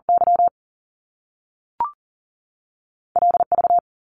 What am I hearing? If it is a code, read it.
Morse code, heard as a keyed tone, about 40 wpm, 700 Hz. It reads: Y L V
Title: Top Two Letter Combinations in Words 40wpm